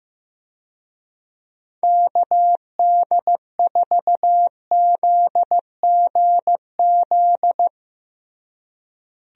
Morse code, heard as a keyed tone, about 15 wpm, 700 Hz. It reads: KD4ZGZ
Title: USA Call Signs - 6 Characters - Repeated 3x - 15wpm